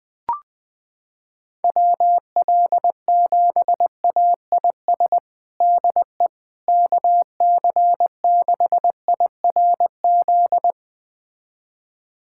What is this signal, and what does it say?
Morse code, heard as a keyed tone, about 20 wpm, 700 Hz. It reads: WL7AIS DE KC6IRZ
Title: USA Callsign DE Callsign 20wpm